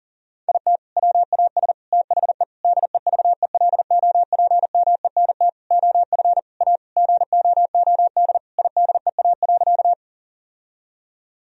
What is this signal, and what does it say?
Morse code, heard as a keyed tone, about 40 wpm, 700 Hz. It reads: IT WAS THE DEVELOPMENT OF A GOOD IDEA.
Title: Sentences from Top 1000 Words - Review Part 4 - 40wpm